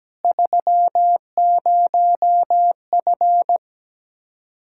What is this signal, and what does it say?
Morse code, heard as a keyed tone, about 17 wpm, 700 Hz. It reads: 30F
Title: Practice QSO - Exchange 2 - Info - 17wpm